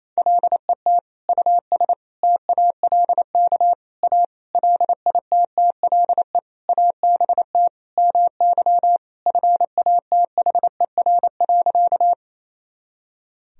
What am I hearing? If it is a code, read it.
Morse code, heard as a keyed tone, about 27 wpm, 700 Hz. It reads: LET US TALK A LITTLE ABT MY FATHER.
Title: Sentences from Top 400 Words and Abbreviations 28wpm